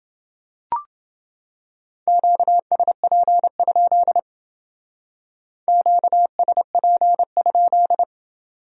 Morse code, heard as a keyed tone, about 29 wpm, 700 Hz. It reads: QSP? QSP?